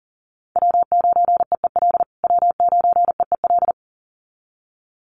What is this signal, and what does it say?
Morse code, heard as a keyed tone, about 40 wpm, 700 Hz. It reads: W9EEL W9EEL
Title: USA Call Signs Repeated 2x 40wpm